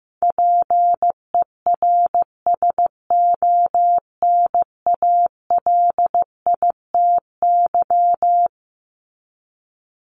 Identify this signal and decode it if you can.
Morse code, heard as a keyed tone, about 14 wpm, 700 Hz. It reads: PERSONALITY
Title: Top 1251-1500 Words 15wpm